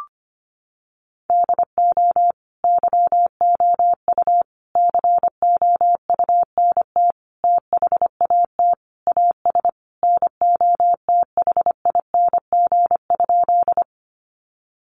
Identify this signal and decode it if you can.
Morse code, heard as a keyed tone, about 25 wpm, 700 Hz. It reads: DO YOU COUNT THAT AS NOTHING?
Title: Sentences from Top 1500 Words 25wpm